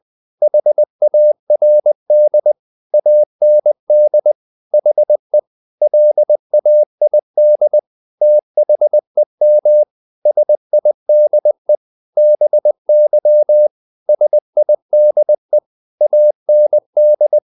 Morse code, heard as a keyed tone, about 20 wpm, 600 Hz. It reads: HARD AND HE LAID THEM SIDE BY SIDE AND
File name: the old man and the sea part 2 20wpm